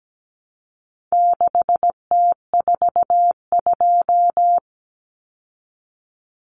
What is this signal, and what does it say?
Morse code, heard as a keyed tone, about 17 wpm, 700 Hz. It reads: 6T42
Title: Common Cut Numbers - 4-Digit 17wpm